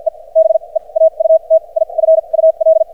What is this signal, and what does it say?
Morse code, heard as a keyed tone, about 54 wpm, spaced as for 36 wpm, 590 Hz. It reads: E BEAUTIFUL
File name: archived / CRSnet